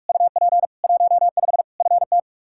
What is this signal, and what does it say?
Morse code, heard as a keyed tone, about 41 wpm, 700 Hz. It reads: UP 15 FT
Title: Practice QSO - Exchange 2 - Info - 45wpm